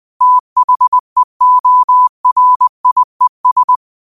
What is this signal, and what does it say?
Morse code, heard as a keyed tone, about 20 wpm, 1000 Hz. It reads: THEORIES
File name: qotdfiles / 20wpm